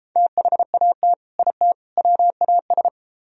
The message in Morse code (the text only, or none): THAT IT WAS